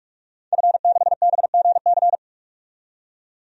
Morse code, heard as a keyed tone, about 47 wpm, 700 Hz. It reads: F6BGC
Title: DXCC-100 Call Signs - 5 Characters - 45wpm